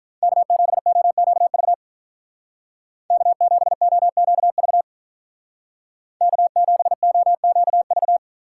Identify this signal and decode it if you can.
Morse code, heard as a keyed tone, about 44 wpm, 700 Hz. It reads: K7OQV K7OQV K7OQV
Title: Speed Racing - USA 1x3 Call Signs 35wpm